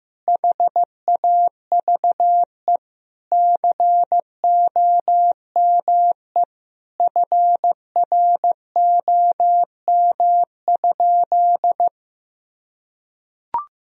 Morse code, heard as a keyed tone, about 16 wpm, 700 Hz. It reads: HAVE COME FROM?
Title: Sentences from Top 1000 Words - Review Part 2 - 15wpm